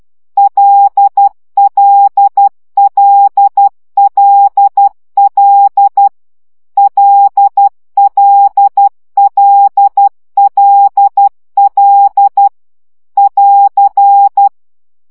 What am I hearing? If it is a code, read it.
Morse code, heard as a keyed tone, about 12 wpm, 800 Hz. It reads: LLLLL LLLLL <AR>
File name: CW gyakorlas l 10 baud